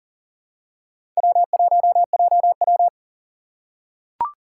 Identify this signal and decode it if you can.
Morse code, heard as a keyed tone, about 40 wpm, 700 Hz. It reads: W1JW
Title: USA Call Signs - 4 Characters - 4x Character Spacing - 40wpm